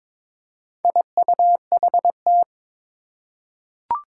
Morse code, heard as a keyed tone, about 23 wpm, 700 Hz. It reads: IUHT
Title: Four Challenging Letter-Number Combinations 22wpm